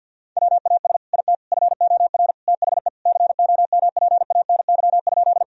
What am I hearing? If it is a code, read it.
Morse code, heard as a keyed tone, about 50 wpm, 700 Hz. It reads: WAS IT FOR THE COMPANY?
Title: Sentences from Top 1000 Words - Review Part 3 - 50wpm